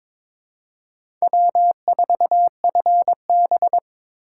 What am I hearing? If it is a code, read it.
Morse code, heard as a keyed tone, about 22 wpm, 700 Hz. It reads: W4FB
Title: USA Call Signs - 3x Character Spacing - 22wpm